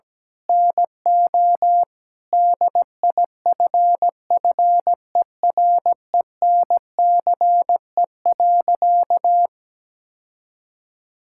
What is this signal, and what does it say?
Morse code, heard as a keyed tone, about 17 wpm, 700 Hz. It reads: NO DIFFERENCE.
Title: Sentences from Top 1000 Words - Review Part 5 - 17wpm